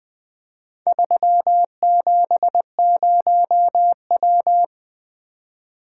Morse code, heard as a keyed tone, about 20 wpm, 700 Hz. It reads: 370W
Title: Four Letter-Number Combinations - 20wpm